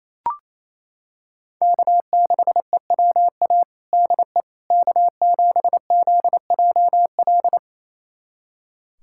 Morse code, heard as a keyed tone, about 28 wpm, 700 Hz. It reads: K6EWA DE K7ZJL
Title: USA Callsign DE Callsign - 5 Characters - 28wpm